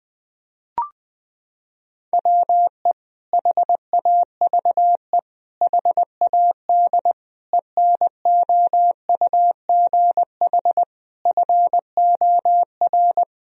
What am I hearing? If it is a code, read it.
Morse code, heard as a keyed tone, about 20 wpm, 700 Hz. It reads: WE HAVE HAD ENOUGH FOR
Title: Sentences from Top 1000 Words - Review Part 3 - 20wpm